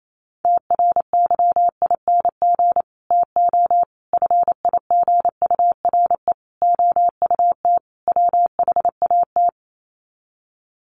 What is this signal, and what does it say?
Morse code, heard as a keyed tone, about 28 wpm, 700 Hz. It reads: TRYING TO FIGURE OUT WHAT
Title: Sets of 5 Words 28wpm